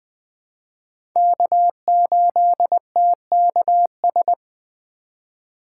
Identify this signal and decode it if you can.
Morse code, heard as a keyed tone, about 20 wpm, 700 Hz. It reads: K8TKS
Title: USA Call Signs - 5 Characters - 4x Character Spacing - 20wpm